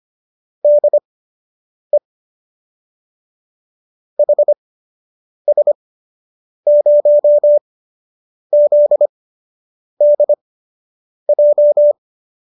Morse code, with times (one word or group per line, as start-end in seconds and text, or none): 0.6-2.0 DE
4.2-11.9 HS0ZDJ